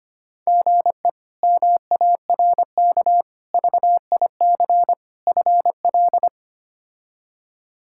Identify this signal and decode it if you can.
Morse code, heard as a keyed tone, about 25 wpm, 700 Hz. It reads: GE MARK VIC FL
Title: K1USN SST QSO Practice - 25wpm